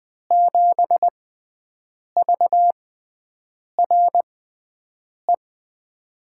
Morse code, heard as a keyed tone, about 20 wpm, spaced as for 7 wpm, 700 Hz. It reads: 7VRE
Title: USA Call Signs - 6 Characters - 6x Character Spacing - 20wpm